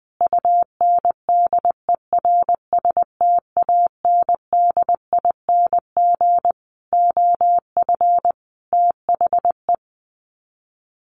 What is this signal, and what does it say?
Morse code, heard as a keyed tone, about 20 wpm, 700 Hz. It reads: UNDERSTANDING OF THE
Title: Sets of 5 Words Encore2 20wpm